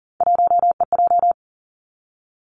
Morse code, heard as a keyed tone, about 38 wpm, 700 Hz. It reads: 1EJ
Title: Three Challenging Letter-Number Combinations 40wpm